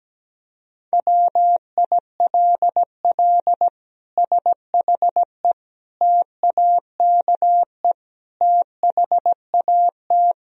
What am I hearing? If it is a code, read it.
Morse code, heard as a keyed tone, about 17 wpm, 700 Hz. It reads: WILL SHE TAKE THAT
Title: Sentences from Top 1000 Words - Review Part 4 - 17wpm